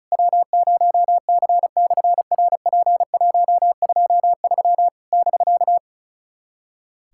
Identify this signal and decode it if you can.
Morse code, heard as a keyed tone, about 36 wpm, 700 Hz. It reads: W0C/RP123 <BK>
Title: SOTA QSO Practice 35wpm